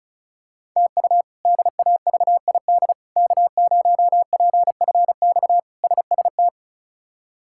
Morse code, heard as a keyed tone, about 36 wpm, 700 Hz. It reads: TU DAVID K0PFX SST
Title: K1USN SST QSO Practice - 35wpm